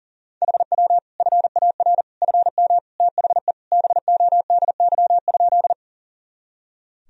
Morse code, heard as a keyed tone, about 40 wpm, 700 Hz. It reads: HW FAR FM THE BODY?
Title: Sentences from Top 400 Words and Abbreviations 40wpm